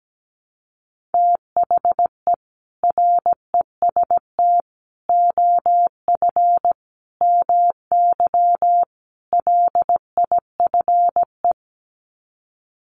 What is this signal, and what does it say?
Morse code, heard as a keyed tone, about 17 wpm, 700 Hz. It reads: THE REST OF MY LIFE
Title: Sets of 5 Words 17wpm